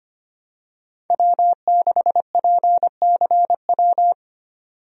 Morse code, heard as a keyed tone, about 25 wpm, 700 Hz. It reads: W6PCW
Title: USA Call Signs - 5 Characters - 25wpm